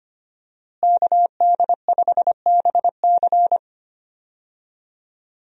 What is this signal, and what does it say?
Morse code, heard as a keyed tone, about 25 wpm, 700 Hz. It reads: KD5BC
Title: USA Call Signs - 5 Characters - Repeated 2x - 25wpm